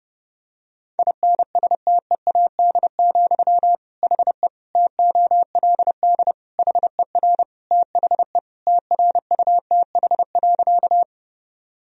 Morse code, heard as a keyed tone, about 32 wpm, 700 Hz. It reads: INSTEAD, HE TOLD HER THE TRUTH.
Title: Sentences from Top 1000 Words - Review Part 4 - 30wpm